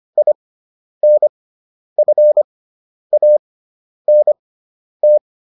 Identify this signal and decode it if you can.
Morse code, heard as a keyed tone, about 25 wpm, spaced as for 10 wpm, 600 Hz. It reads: INFANT